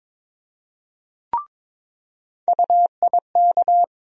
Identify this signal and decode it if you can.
Morse code, heard as a keyed tone, about 21 wpm, 700 Hz. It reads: UIK
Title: USA Call Sign Postfixes - 3 Letters 22wpm